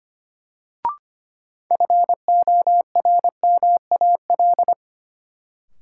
Morse code, heard as a keyed tone, about 25 wpm, 700 Hz. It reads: FORMAL
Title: Sentences from Top 1300 Words - New Word Drill - 25wpm